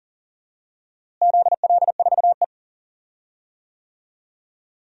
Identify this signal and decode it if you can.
Morse code, heard as a keyed tone, about 40 wpm, 700 Hz. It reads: ZL4E
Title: DXCC-100 Call Signs - 4 Characters - Repeated 2x - 40wpm